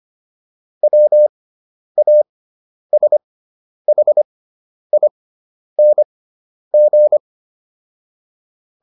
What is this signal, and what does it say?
Morse code, heard as a keyed tone, about 25 wpm, spaced as for 10 wpm, 600 Hz. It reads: WASHING